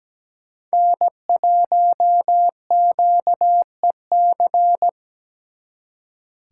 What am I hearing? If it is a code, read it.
Morse code, heard as a keyed tone, about 20 wpm, 700 Hz. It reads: N1QEC
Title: Speed Racing - USA Call Signs 17wpm